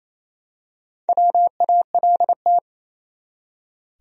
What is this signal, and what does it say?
Morse code, heard as a keyed tone, about 27 wpm, 700 Hz. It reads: WALT
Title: USA Common Names - Max 4 Letters - 28wpm